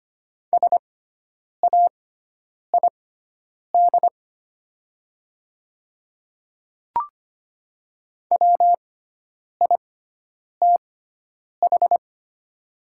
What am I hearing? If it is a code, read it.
Morse code, heard as a keyed tone, about 25 wpm, spaced as for 9 wpm, 700 Hz. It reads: SAID WITH
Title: 6x Character Spacing - Top 100 Words - 25wpm